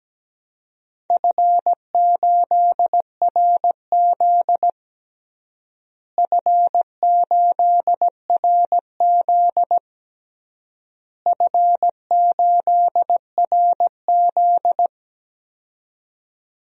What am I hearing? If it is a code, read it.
Morse code, heard as a keyed tone, about 17 wpm, 700 Hz. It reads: F8RZ F8RZ F8RZ
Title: DXCC-100 Call Signs - 4 Characters - Repeated 3x - 17wpm